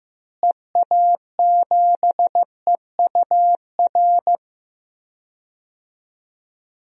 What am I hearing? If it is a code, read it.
Morse code, heard as a keyed tone, about 15 wpm, 700 Hz. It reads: EA7EUR
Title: DXCC-100 Call Signs - 6 Characters - Repeated 2x - 15wpm